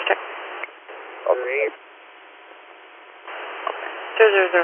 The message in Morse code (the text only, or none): none